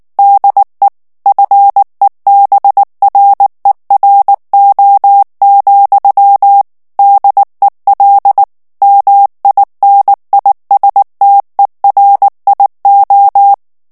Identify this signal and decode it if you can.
Morse code, heard as a keyed tone, about 19 wpm, 800 Hz. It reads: DE FEBRERO, DEL MINISTERIO
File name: morse mp3 / ESPAGNOL TEXTE EN CLAIR 2 VITESSE 18 MOTS MINUTES